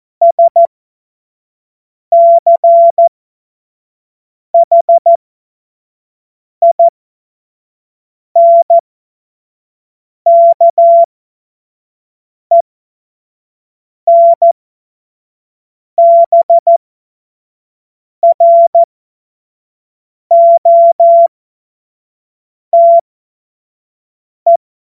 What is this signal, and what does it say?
Morse code, heard as a keyed tone, about 14 wpm, spaced as for 5 wpm, 680 Hz. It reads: SCHINKENBROTE
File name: Witz-16-September-2025-25-F70